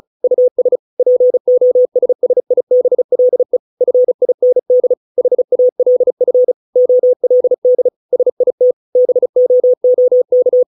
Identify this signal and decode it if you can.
Morse code, heard as a keyed tone, about 36 wpm, 500 Hz. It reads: US POSSIBLE FIND HALF OLD SIT BOOK